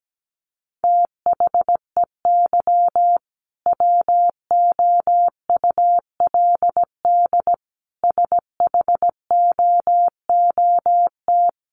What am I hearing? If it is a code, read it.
Morse code, heard as a keyed tone, about 17 wpm, 700 Hz. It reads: THEY WOULD SHOOT
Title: Sentences from Top 1200 Words 17wpm